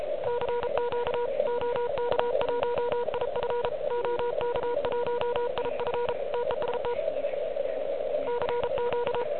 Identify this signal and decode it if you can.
Morse code, heard as a keyed tone, about 34 wpm, 970 Hz, with noise. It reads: CQ OK1IF OK1IF TEST CQ
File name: ARCHIV / QCX / QCX Pileup